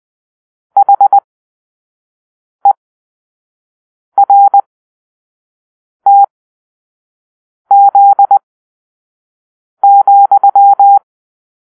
Morse code, heard as a keyed tone, about 20 wpm, spaced as for 5 wpm, 800 Hz. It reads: HERTZ,
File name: arrl / Grp-5 Drill-4